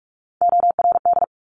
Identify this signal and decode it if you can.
Morse code, heard as a keyed tone, about 46 wpm, 690 Hz. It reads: ORD
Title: Top Three Letter Combinations in Words 45wpm